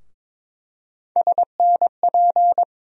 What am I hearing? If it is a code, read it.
Morse code, heard as a keyed tone, about 22 wpm, 700 Hz. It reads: SNP
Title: USA Call Sign Postfixes - 3 Letters 22wpm